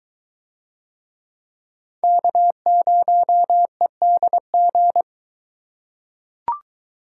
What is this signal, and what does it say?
Morse code, heard as a keyed tone, about 22 wpm, 700 Hz. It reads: K0EDG